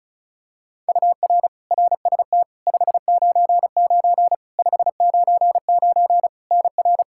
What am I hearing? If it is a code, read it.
Morse code, heard as a keyed tone, about 36 wpm, 700 Hz. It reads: UR RST 599 599 NR
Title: Practice QSO - Exchange 1 - Beginning - 35wpm